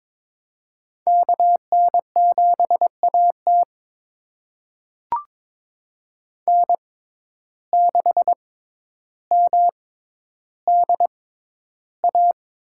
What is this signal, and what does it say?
Morse code, heard as a keyed tone, about 23 wpm, spaced as for 11 wpm, 700 Hz. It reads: KN7AT N6MDA
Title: USA Call Signs - 5 Characters - 6x Character Spacing - 22wpm